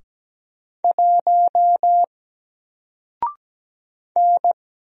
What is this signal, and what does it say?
Morse code, heard as a keyed tone, about 17 wpm, 700 Hz. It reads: 1 N
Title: Single Letter-Number 17wpm